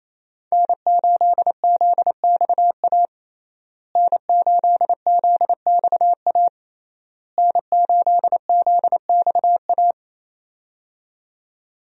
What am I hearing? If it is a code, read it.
Morse code, heard as a keyed tone, about 28 wpm, 700 Hz. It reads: N8ZXA N8ZXA N8ZXA
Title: USA Call Signs - 5 Characters - Repeated 3x - 28wpm